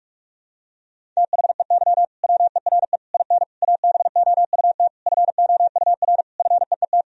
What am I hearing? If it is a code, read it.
Morse code, heard as a keyed tone, about 46 wpm, 700 Hz. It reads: THEY WERE IN ABOUT FOUR FEET
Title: Sentences from Top 1000 Words - Review Part 1 - 45wpm